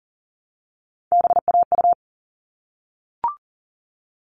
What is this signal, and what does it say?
Morse code, heard as a keyed tone, about 40 wpm, 700 Hz. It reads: BAU
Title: All Cut Numbers - 3-Digit 40wpm